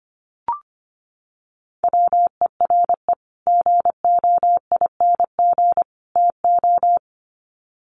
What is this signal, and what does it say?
Morse code, heard as a keyed tone, about 25 wpm, 700 Hz. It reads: WERE GOING TO
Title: Sets of 3 Words 25wpm